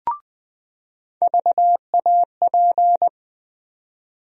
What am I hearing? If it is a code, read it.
Morse code, heard as a keyed tone, about 20 wpm, 700 Hz. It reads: VAP
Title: USA Call Sign Postfixes - 3 Letters 20wpm